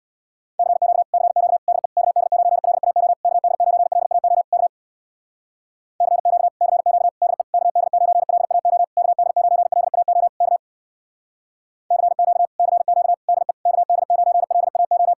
Morse code, heard as a keyed tone, about 58 wpm, 700 Hz. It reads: CQ CQ DE KD9LAO KD9LAO K CQ CQ DE KD9LAO KD9LAO K CQ CQ DE KD9LAO
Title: Speed-Racing Common QSO Phrases - 50wpm